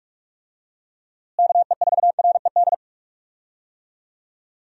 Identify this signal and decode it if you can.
Morse code, heard as a keyed tone, about 46 wpm, 700 Hz. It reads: KE4RED
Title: USA Call Signs - 6 Characters - Repeated 3x - 45wpm